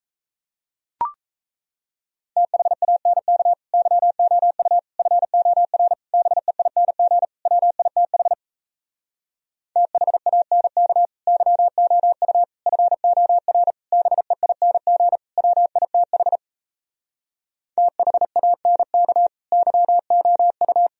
Morse code, heard as a keyed tone, about 38 wpm, 700 Hz. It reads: THANK YOU FOR BEING WITH THANK YOU FOR BEING WITH THANK YOU